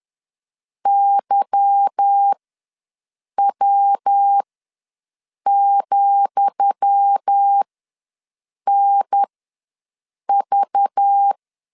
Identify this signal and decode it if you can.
Morse code, heard as a keyed tone, about 10 wpm, spaced as for 6 wpm, 800 Hz. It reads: YW,NV
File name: LEZIONI / 10 SIGNS / 10D COMA ws 10wpm 4314